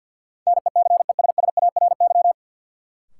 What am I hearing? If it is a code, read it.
Morse code, heard as a keyed tone, about 53 wpm, 700 Hz. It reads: NECESSARY
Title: Sentences from Top 800 Words - New Word Drill - 50wpm